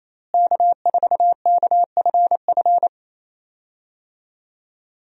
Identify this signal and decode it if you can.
Morse code, heard as a keyed tone, about 28 wpm, 700 Hz. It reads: K4KFF
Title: USA Call Signs - 5 Characters - Repeated 3x - 28wpm